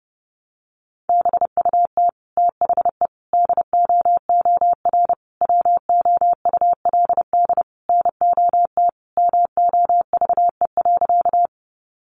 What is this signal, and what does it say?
Morse code, heard as a keyed tone, about 30 wpm, 700 Hz. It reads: BUT THE DOOR WOULD NOT MOVE.
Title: Sentences from Top 600 Words 30wpm